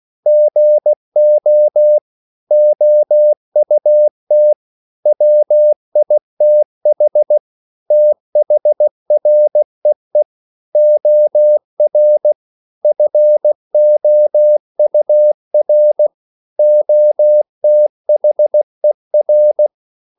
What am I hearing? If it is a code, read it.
Morse code, heard as a keyed tone, about 16 wpm, 600 Hz. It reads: GO OUT WITH THREE OR FOUR OTHER